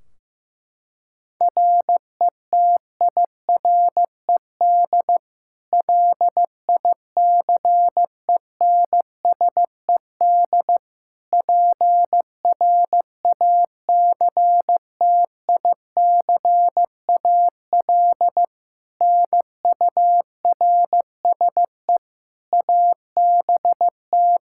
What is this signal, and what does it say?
Morse code, heard as a keyed tone, about 15 wpm, 700 Hz. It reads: RETIRED LICENSED PRACTICAL NURSE ABT
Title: Practice QSO - Exchange 3 - More Info - 15wpm